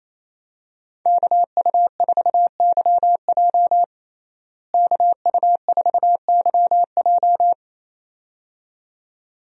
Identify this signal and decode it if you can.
Morse code, heard as a keyed tone, about 28 wpm, 700 Hz. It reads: KU4YJ KU4YJ
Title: USA Call Signs - 5 Characters - Repeated 2x - 28wpm